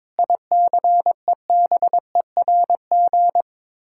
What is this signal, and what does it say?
Morse code, heard as a keyed tone, about 22 wpm, 700 Hz. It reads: ICEBERG
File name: practice / 22wpm / Sets of 7 Words - 22wpm